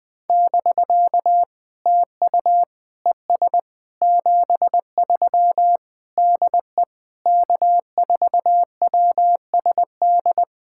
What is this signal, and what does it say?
Morse code, heard as a keyed tone, about 20 wpm, 700 Hz. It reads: <BK> TU ES 73 DE K4WSD
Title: POTA QSO Practice - 20wpm